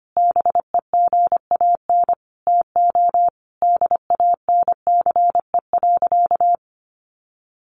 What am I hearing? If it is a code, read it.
Morse code, heard as a keyed tone, about 25 wpm, 700 Hz. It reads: BEGAN TO DANCE.
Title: Sentences from Top 900 Words 25wpm